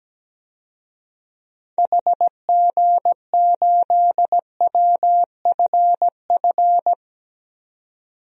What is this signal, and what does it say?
Morse code, heard as a keyed tone, about 17 wpm, 700 Hz. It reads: HG8WFF
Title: DXCC-100 Call Signs - 6 Characters - 17wpm